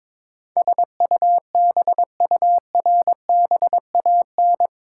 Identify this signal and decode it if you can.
Morse code, heard as a keyed tone, about 22 wpm, 700 Hz. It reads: SUBURBAN
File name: practice / 22wpm / Eight Letter Words - 22wpm